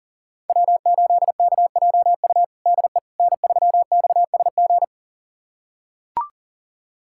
Morse code, heard as a keyed tone, about 37 wpm, 700 Hz. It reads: W8KJU DE N3XSG
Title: USA Callsign DE Callsign - 5 Characters - 40wpm